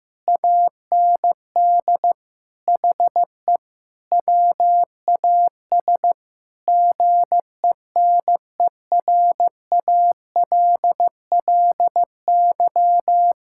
Morse code, heard as a keyed tone, about 15 wpm, 700 Hz. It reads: AND HE WAS GENERALLY
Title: Sentences from Top 1000 Words - Review Part 6 - 15wpm